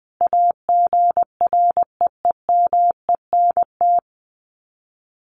Morse code, heard as a keyed tone, about 19 wpm, 700 Hz. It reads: AGREEMENT
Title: Sets of 5 Words Encore 20wpm